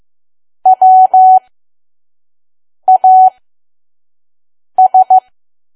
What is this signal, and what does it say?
Morse code, heard as a keyed tone, about 15 wpm, spaced as for 5 wpm, 750 Hz. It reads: WAS
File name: Archive / 5 WPM files / PM